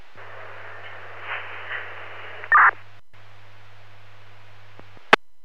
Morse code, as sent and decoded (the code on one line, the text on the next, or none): none